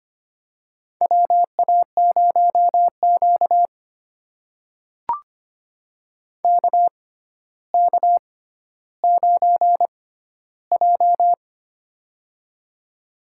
Morse code, heard as a keyed tone, about 25 wpm, spaced as for 13 wpm, 700 Hz. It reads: WA0Q KK9J
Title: USA Call Signs - 4 Characters - 6x Character Spacing - 25wpm